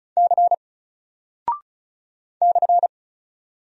Morse code, heard as a keyed tone, about 34 wpm, 700 Hz. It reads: C /
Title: Single Character (Letter:Number:Punctuation:Prosign) Alt 35wpm